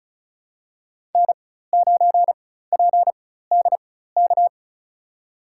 Morse code, heard as a keyed tone, about 36 wpm, spaced as for 16 wpm, 700 Hz. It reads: N9PDK
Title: USA Call Signs - 5 Characters - 4x Character Spacing - 35wpm